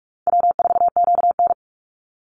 Morse code, heard as a keyed tone, about 46 wpm, 700 Hz. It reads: W4QN
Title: Easy Call Signs - 45wpm